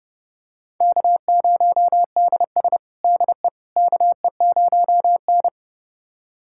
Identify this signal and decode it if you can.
Morse code, heard as a keyed tone, about 32 wpm, 700 Hz. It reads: K0DS DE KE0N
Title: USA Callsign DE Callsign - 4 Characters - 30wpm